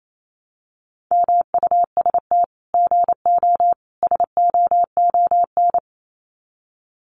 Morse code, heard as a keyed tone, about 28 wpm, 700 Hz. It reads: MUST GO SOON
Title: Common QSO Phrases 28wpm